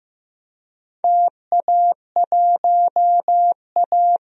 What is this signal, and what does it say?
Morse code, heard as a keyed tone, about 15 wpm, 700 Hz. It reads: TA1A
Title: DXCC-100 Call Signs - 4 Characters - Repeated 3x - 15wpm